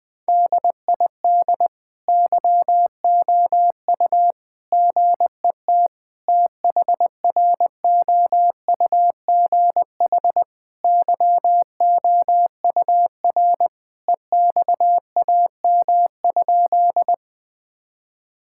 Morse code, heard as a keyed tone, about 20 wpm, 700 Hz. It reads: DID YOU GET THROUGH YOUR EXAM?
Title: Sentences from Top 1000 Words - Review Part 5 - 20wpm